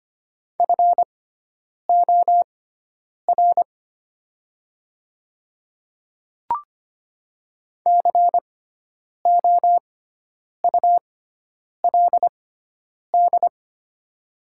Morse code, heard as a keyed tone, about 25 wpm, spaced as for 9 wpm, 690 Hz. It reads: FOR COULD
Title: 6x Character Spacing - Top 100 Words - 25wpm